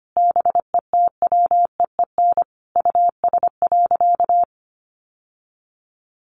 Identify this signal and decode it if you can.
Morse code, heard as a keyed tone, about 25 wpm, 700 Hz. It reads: BETWEEN US.
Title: Sentences from Top 800 Words 25wpm